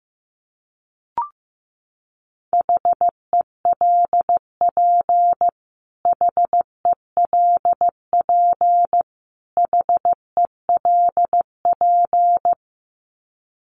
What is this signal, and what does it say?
Morse code, heard as a keyed tone, about 15 wpm, 700 Hz. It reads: HELP HELP HELP
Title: Top 200 Words Repeated 15wpm